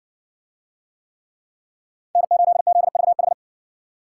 Morse code, heard as a keyed tone, about 52 wpm, 700 Hz. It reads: N8ZVH